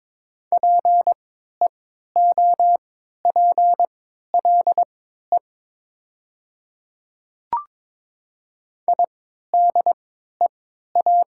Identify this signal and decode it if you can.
Morse code, heard as a keyed tone, about 22 wpm, spaced as for 13 wpm, 700 Hz. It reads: PEOPLE IDEA
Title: 3x Character Spacing - Top 300 Words - 22wpm